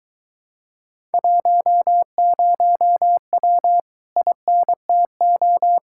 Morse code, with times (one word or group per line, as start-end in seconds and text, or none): 1.1-3.8 10W
4.2-5.8 INTO